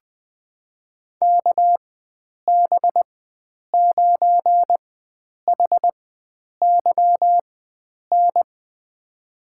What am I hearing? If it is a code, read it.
Morse code, heard as a keyed tone, about 20 wpm, spaced as for 9 wpm, 700 Hz. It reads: KB9HYN
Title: USA Call Signs - 6 Characters - 4x Character Spacing - 20wpm